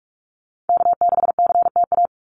Morse code, heard as a keyed tone, about 44 wpm, 700 Hz. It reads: K6CTA
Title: Call Signs Repeated 2x - 45wpm